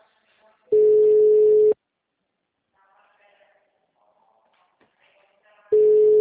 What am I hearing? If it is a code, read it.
Ringback tone (Europe, 425 Hz).
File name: out- -2020- - - .63478